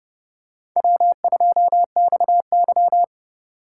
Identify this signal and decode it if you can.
Morse code, heard as a keyed tone, about 28 wpm, 700 Hz. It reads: W2XY